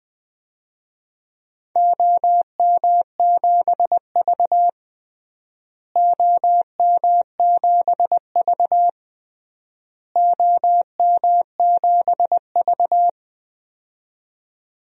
Morse code, heard as a keyed tone, about 20 wpm, 700 Hz. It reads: OM7V OM7V OM7V
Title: DXCC-100 Call Signs - 4 Characters - Repeated 3x - 20wpm